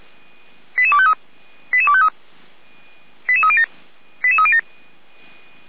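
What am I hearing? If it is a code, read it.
Five-tone selective call (ZVEI), heard as quick runs of five tones, tones of about 70 ms. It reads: 88253 88253 88307 88307
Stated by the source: ZVEI five-tone selective call received on SDR.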